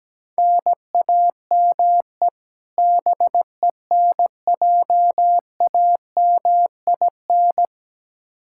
Morse code, heard as a keyed tone, about 17 wpm, 700 Hz. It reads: NAME BENJAMIN
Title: K1USN SST QSO Practice 17wpm